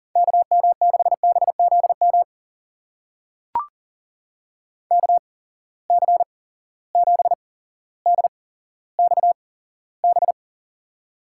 Morse code, heard as a keyed tone, about 39 wpm, spaced as for 17 wpm, 700 Hz. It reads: KM6BZM KC7DXB
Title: USA Call Signs - 6 Characters - 8x Character Spacing - 40wpm